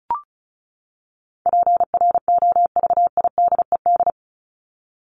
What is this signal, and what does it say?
Morse code, heard as a keyed tone, about 34 wpm, 700 Hz. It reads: PROVIDED
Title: Top 1751-2000 Words 35wpm